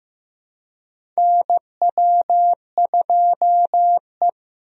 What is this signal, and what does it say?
Morse code, heard as a keyed tone, about 15 wpm, 700 Hz. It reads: NW2E
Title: USA Call Signs - 4 Characters - Repeated 3x - 15wpm